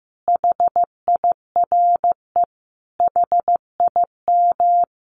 Morse code, heard as a keyed tone, about 15 wpm, 700 Hz. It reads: HIRE HIM
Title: Sentences from Top 1400 Words 15wpm